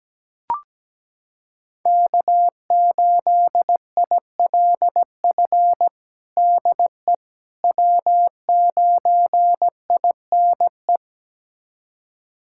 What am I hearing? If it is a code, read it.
Morse code, heard as a keyed tone, about 17 wpm, 700 Hz. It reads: K8ILF DE W9INE
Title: USA Callsign DE Callsign - 5 Characters - 17wpm